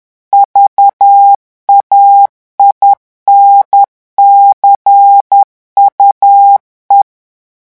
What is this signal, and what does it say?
Morse code, heard as a keyed tone, about 10 wpm, 800 Hz. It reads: VAINCUE
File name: revision9 citations humoristiques vitesse 10 mots minute